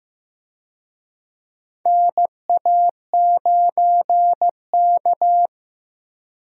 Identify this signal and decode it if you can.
Morse code, heard as a keyed tone, about 16 wpm, 700 Hz. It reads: NA9K
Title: USA Call Signs - 4 Characters - Repeated 2x - 15wpm